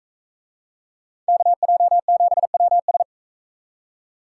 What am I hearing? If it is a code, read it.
Morse code, heard as a keyed tone, about 40 wpm, 700 Hz. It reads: KJ7WS